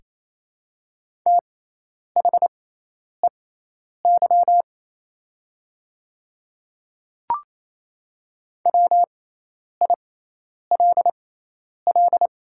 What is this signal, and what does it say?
Morse code, heard as a keyed tone, about 28 wpm, spaced as for 10 wpm, 700 Hz. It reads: THEY WILL
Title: 6x Character Spacing - Top 100 Words - 28wpm